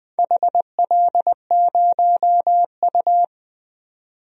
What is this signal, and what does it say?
Morse code, heard as a keyed tone, about 20 wpm, 700 Hz. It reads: HL0U
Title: DXCC-100 Call Signs - 4 Characters - 20wpm